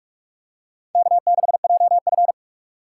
Morse code, heard as a keyed tone, about 42 wpm, 700 Hz. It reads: K6JF
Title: USA Call Signs - 4 Characters - 4x Character Spacing - 45wpm